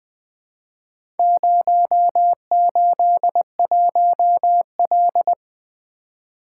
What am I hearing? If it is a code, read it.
Morse code, heard as a keyed tone, about 20 wpm, 700 Hz. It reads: 081L
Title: Four Letter-Number Combinations - 20wpm